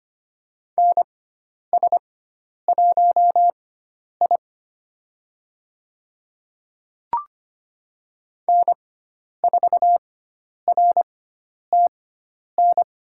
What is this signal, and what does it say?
Morse code, heard as a keyed tone, about 25 wpm, spaced as for 10 wpm, 700 Hz. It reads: NS1I N4RTN
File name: Lesson 16g - Letter c - Call Signs 25wpm